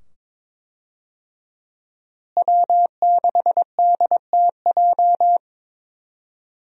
Morse code, heard as a keyed tone, about 22 wpm, 700 Hz. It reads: W6DTJ
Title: USA Call Signs - 5 Characters - Repeated 2x - 22wpm